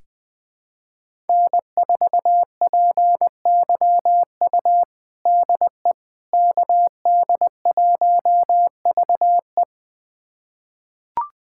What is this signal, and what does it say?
Morse code, heard as a keyed tone, about 20 wpm, 700 Hz. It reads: N4PYU DE KD1VE
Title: USA Callsign DE Callsign - 5 Characters - 20wpm